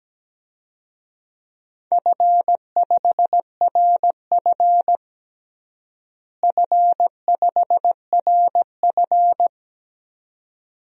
Morse code, heard as a keyed tone, about 17 wpm, 700 Hz. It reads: F5RF F5RF
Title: DXCC-100 Call Signs - 4 Characters - Repeated 2x - 17wpm